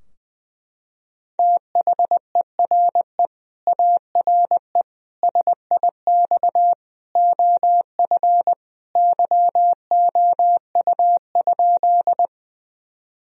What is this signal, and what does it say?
Morse code, heard as a keyed tone, about 20 wpm, 700 Hz. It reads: THERE ARE SIX OF YOU?
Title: Sentences from Top 1000 Words - Review Part 2 - 20wpm